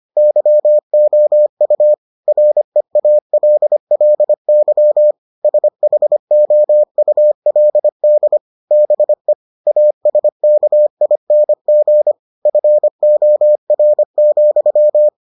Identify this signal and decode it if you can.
Morse code, heard as a keyed tone, about 25 wpm, 600 Hz. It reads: YOU REALLY SHOULD BE ASKING FOR,